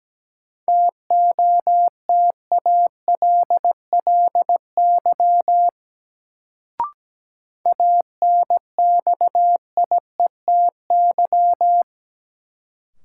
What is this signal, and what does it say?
Morse code, heard as a keyed tone, about 17 wpm, 700 Hz. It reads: TOTALLY ANXIETY
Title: Sentences from Top 1600 Words - New Word Drill - 17wpm